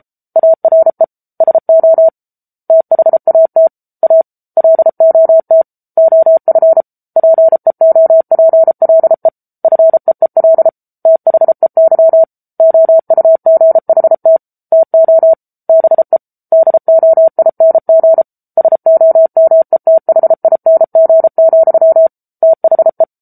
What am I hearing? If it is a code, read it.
Morse code, heard as a keyed tone, about 34 wpm, 650 Hz. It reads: ARE SO THAT A LOT OF PEOPLE FEEL THEY OUGHT TO BE DOING SOMETHING, THE
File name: Book Two / 07-Chapter 7 - 33 WPM